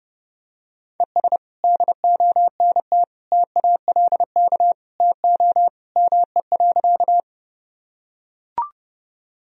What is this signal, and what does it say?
Morse code, heard as a keyed tone, about 32 wpm, 700 Hz. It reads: ES DONT TALK TO ME.
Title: Sentences from Top 400 Words and Abbreviations 30wpm